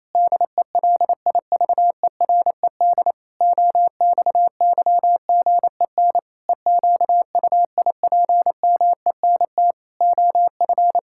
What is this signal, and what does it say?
Morse code, heard as a keyed tone, about 28 wpm, 700 Hz. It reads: DELIVERED OXYGEN EQUIPMENT OF
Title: Pangram Challenge v2 - 28wpm